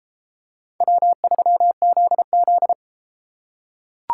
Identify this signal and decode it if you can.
Morse code, heard as a keyed tone, about 33 wpm, 700 Hz. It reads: W3ZZ